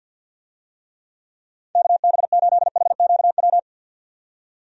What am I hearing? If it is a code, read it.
Morse code, heard as a keyed tone, about 52 wpm, 700 Hz. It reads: KB8HQW
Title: USA Call Signs - 6 Characters - 50wpm